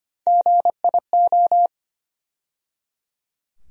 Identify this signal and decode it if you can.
Morse code, heard as a keyed tone, about 25 wpm, 700 Hz. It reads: GIO
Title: USA Call Sign Postfixes - 3 Letters 25wpm